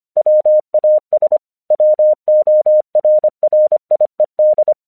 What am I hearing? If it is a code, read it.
Morse code, heard as a keyed tone, about 25 wpm, 610 Hz. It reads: WAS WORRIED
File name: 1 English 25wpm / AQC14